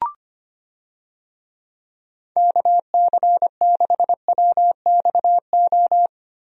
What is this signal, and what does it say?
Morse code, heard as a keyed tone, about 25 wpm, 700 Hz. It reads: KC6WXO
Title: USA Call Signs - 6 Characters - Repeated 2x - 25wpm